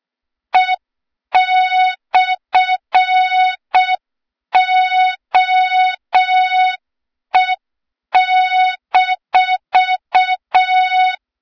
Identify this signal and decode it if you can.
Morse code, heard as a keyed tone, about 6 wpm, 3700 Hz. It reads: E/OE-
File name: Morze-HA2MN / hang / 23 GYAKORLAT